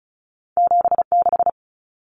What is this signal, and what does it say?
Morse code, heard as a keyed tone, about 36 wpm, 700 Hz. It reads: 76
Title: Numbers - 2-Digit 35wpm